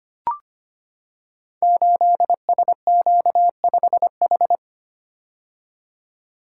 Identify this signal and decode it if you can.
Morse code, heard as a keyed tone, about 25 wpm, 700 Hz. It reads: 8SQ5H
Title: Code Groups - 5 Letter:Numbers - 25wpm